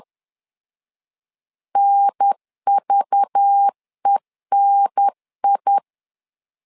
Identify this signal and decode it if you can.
Morse code, heard as a keyed tone, about 10 wpm, 800 Hz. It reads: NVENI
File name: LEZIONI / 5 LBVC / 5 LBV 10wpm 2248